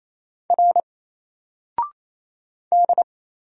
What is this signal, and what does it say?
Morse code, heard as a keyed tone, about 28 wpm, 700 Hz. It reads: R D
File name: Single-Letter-Number-28wpm